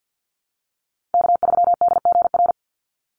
Morse code, heard as a keyed tone, about 50 wpm, 700 Hz. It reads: K3DGR
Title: Call Signs Repeated 2x - 50wpm